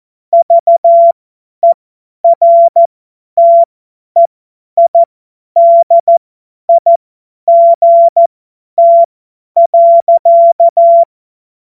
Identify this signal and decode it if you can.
Morse code, heard as a keyed tone, about 14 wpm, spaced as for 10 wpm, 680 Hz. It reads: VERTEIDIGT.
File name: Morsekurs / Witze / Witz 10. Mai 2022 50 F70